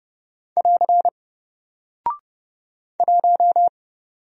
Morse code, heard as a keyed tone, about 30 wpm, 700 Hz. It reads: <AR> 1
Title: Single Character (Letter:Number:Punctuation:Prosign) 30wpm